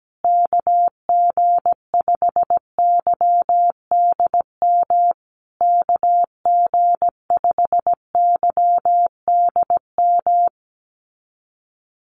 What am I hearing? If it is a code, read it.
Morse code, heard as a keyed tone, about 17 wpm, 700 Hz. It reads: KG5YDM KG5YDM
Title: USA Call Signs Repeated 3x 17wpm